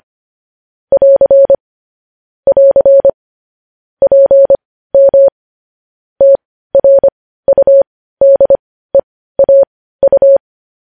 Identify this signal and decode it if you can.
Morse code, heard as a keyed tone, about 25 wpm, spaced as for 15 wpm, 550 Hz. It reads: <AR> <AR> PM TRUDEAU
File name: podcast / News headlines-1x-25x15